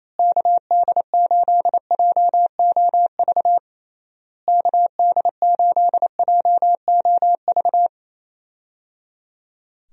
Morse code, heard as a keyed tone, about 28 wpm, 700 Hz. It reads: KD8JOV KD8JOV
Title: USA Call Signs - 6 Characters - Repeated 3x - 28wpm